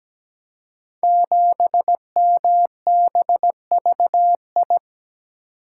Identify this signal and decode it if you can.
Morse code, heard as a keyed tone, about 17 wpm, 690 Hz. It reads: 7MBVI
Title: Code Groups - 5 Letter:Numbers - 17wpm